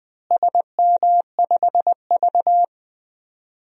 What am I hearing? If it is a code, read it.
Morse code, heard as a keyed tone, about 20 wpm, 700 Hz. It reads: SM5V
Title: DXCC-100 Call Signs - 4 Characters - Repeated 2x - 20wpm